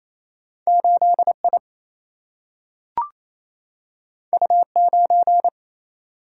Morse code, heard as a keyed tone, about 28 wpm, 700 Hz. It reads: 8 I U 9
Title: Single-Letter Single-Number 28wpm